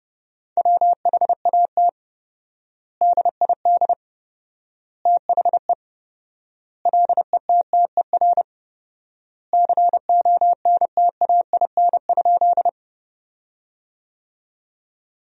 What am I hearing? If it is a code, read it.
Morse code, heard as a keyed tone, about 32 wpm, 700 Hz. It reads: WHAT DID THE LETTER CONTAIN?
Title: 4x Word-Spacing - Sentences from Top 500 Words - 30wpm